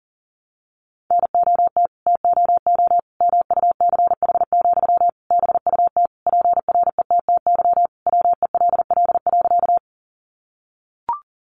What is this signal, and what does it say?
Morse code, heard as a keyed tone, about 40 wpm, 700 Hz. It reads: NOT TOO MUCH, BUT PRETTY WELL.
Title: Sentences from Top 1000 Words 40wpm